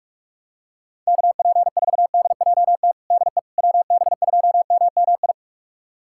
Morse code, heard as a keyed tone, about 41 wpm, 700 Hz. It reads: KP4DJT DE WB2MMI
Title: USA Callsign DE Callsign - 6 Characters - 45wpm